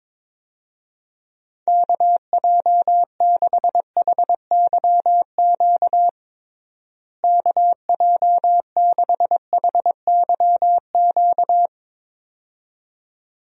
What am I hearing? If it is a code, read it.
Morse code, heard as a keyed tone, about 22 wpm, 700 Hz. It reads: KJ6HYQ KJ6HYQ
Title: USA Call Signs - 6 Characters - Repeated 2x - 22wpm